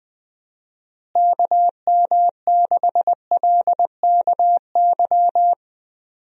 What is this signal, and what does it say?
Morse code, heard as a keyed tone, about 20 wpm, 700 Hz. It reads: KM6LKY
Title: USA Call Signs - 6 Characters - Repeated 2x - 20wpm